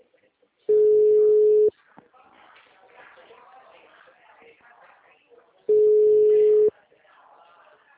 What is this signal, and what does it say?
Ringback tone (Europe, 425 Hz), with speech.